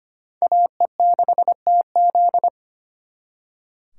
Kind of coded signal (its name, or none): Morse code